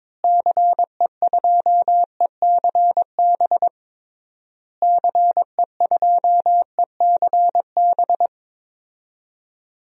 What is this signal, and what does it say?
Morse code, heard as a keyed tone, about 22 wpm, 700 Hz. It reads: CE2ECB CE2ECB
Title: DXCC-100 Call Signs - 6 Characters - Repeated 3x - 22wpm